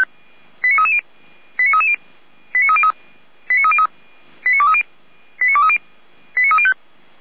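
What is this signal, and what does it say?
Five-tone selective call (ZVEI), heard as quick runs of five tones, tones of about 70 ms. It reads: 89330 89330 89303 89303 89230 89230 89305
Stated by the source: ZVEI five-tone selective call received via SDR, from the audio archive of the Feuerwehr Orlamünde fire brigade.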